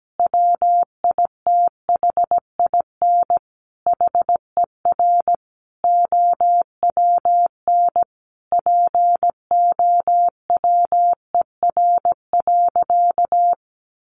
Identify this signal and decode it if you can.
Morse code, heard as a keyed tone, about 17 wpm, 700 Hz. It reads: WITHIN HER OWN POWER.
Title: Sentences from Top 1200 Words 17wpm